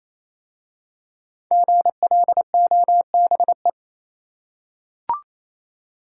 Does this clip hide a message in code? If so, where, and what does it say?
Yes: Morse code from 1.5 to 3.7 s, heard as a keyed tone, about 28 wpm, 690 Hz. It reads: GLOBE